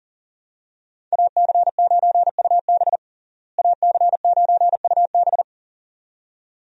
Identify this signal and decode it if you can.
Morse code, heard as a keyed tone, about 37 wpm, 700 Hz. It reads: AC9UB AC9UB
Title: USA Call Signs - 5 Characters - Repeated 2x - 40wpm